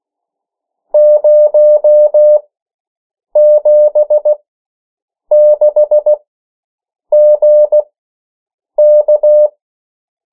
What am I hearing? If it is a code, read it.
Morse code, heard as a keyed tone, about 16 wpm, spaced as for 7 wpm, 600 Hz. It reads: 076GK
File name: morse mp3 / 14 urok 13